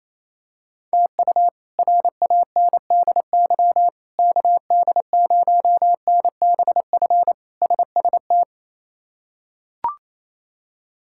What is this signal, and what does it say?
Morse code, heard as a keyed tone, about 27 wpm, 700 Hz. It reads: TU RANDY KD0NBF SST
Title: K1USN SST QSO Practice - 28wpm